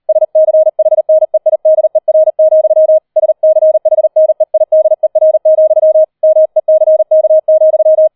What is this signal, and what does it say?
Morse code, heard as a keyed tone, about 38 wpm, 630 Hz. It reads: SCHNEIDER, SCHNEIDER, MECK,